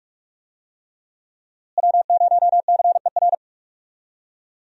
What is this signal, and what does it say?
Morse code, heard as a keyed tone, about 44 wpm, 700 Hz. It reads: W0CER
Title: USA Call Signs - 5 Characters - Repeated 3x - 45wpm